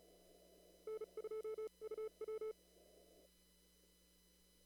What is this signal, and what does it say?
Morse code, heard as a keyed tone, about 36 wpm, spaced as for 30 wpm, 440 Hz. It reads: N2UW
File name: CW / kt5j wpx cw-op-k5pi